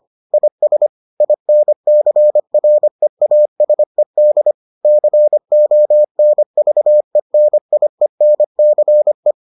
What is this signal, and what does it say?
Morse code, heard as a keyed tone, about 25 wpm, 600 Hz. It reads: IS INCREASED CONVENIENCE